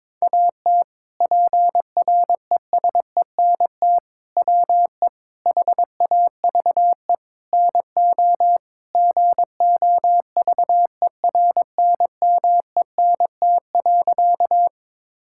Morse code, heard as a keyed tone, about 22 wpm, 700 Hz. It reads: AT PRESENT WE HAVE NO GOVERNMENT.
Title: Sentences from Top 1000 Words - Review Part 5 - 22wpm